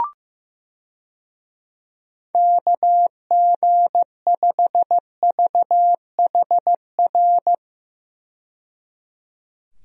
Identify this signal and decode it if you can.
Morse code, heard as a keyed tone, about 16 wpm, 700 Hz. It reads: KG5VHR
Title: USA Call Signs - 6 Characters - 15wpm